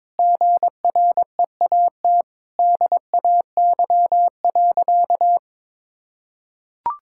Morse code, heard as a keyed tone, about 22 wpm, 700 Hz. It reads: GREAT DAY.
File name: practice / 22wpm / Sentences with 4-Word Phrases - 22wpm